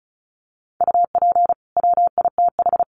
Morse code, heard as a keyed tone, about 34 wpm, 700 Hz. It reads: UP WITH
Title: Sets of 2 Words 35wpm